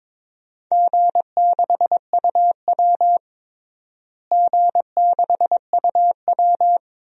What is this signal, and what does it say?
Morse code, heard as a keyed tone, about 21 wpm, 700 Hz. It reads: G6UW G6UW
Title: DXCC-100 Call Signs - 4 Characters - Repeated 2x - 22wpm